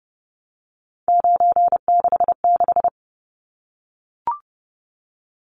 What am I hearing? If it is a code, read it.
Morse code, heard as a keyed tone, about 30 wpm, 700 Hz. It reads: 966
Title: Numbers - 3-Digit 30wpm